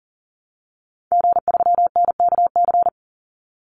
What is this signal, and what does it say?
Morse code, heard as a keyed tone, about 40 wpm, 700 Hz. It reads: G3NKC
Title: Call Signs Repeated 3x - 40wpm